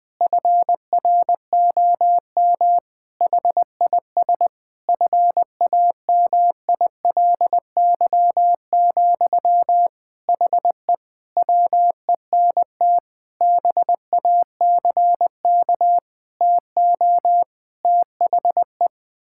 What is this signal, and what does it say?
Morse code, heard as a keyed tone, about 21 wpm, 700 Hz. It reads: FROM HIS FAMILY, HE WENT BACK TO THE
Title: Sentences from Top 1000 Words - Review Part 6 - 20wpm